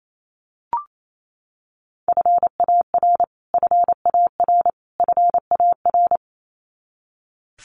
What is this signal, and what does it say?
Morse code, heard as a keyed tone, about 28 wpm, 700 Hz. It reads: FAR FAR FAR
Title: Top 300 Words Repeated 28wpm